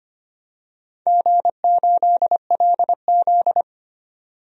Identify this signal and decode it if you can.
Morse code, heard as a keyed tone, about 25 wpm, 700 Hz. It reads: G8LZ
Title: DXCC-100 Call Signs - 4 Characters - 25wpm